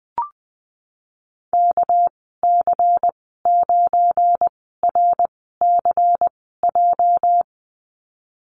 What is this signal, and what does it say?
Morse code, heard as a keyed tone, about 20 wpm, spaced as for 15 wpm, 700 Hz. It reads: KC9RCJ
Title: 219 Lesson 35g - Number 0 - Call Signs 20wpm